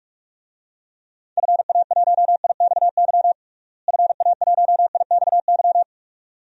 Morse code, heard as a keyed tone, about 41 wpm, 700 Hz. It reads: FA1IXY FA1IXY
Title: DXCC-100 Call Signs - 6 Characters - Repeated 2x - 45wpm